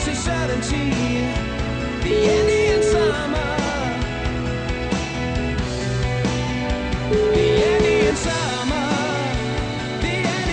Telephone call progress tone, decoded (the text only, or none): ringback tone